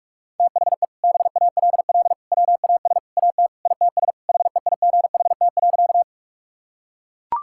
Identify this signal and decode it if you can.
Morse code, heard as a keyed tone, about 41 wpm, 700 Hz. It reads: THE BALL WAS AT ITS HEIGHT.
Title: Sentences from Top 1600 Words 45wpm